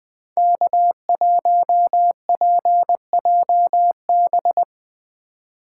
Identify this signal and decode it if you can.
Morse code, heard as a keyed tone, about 22 wpm, 700 Hz. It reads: K1PJB
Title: Speed Racing - USA 1x3 Call Signs 20wpm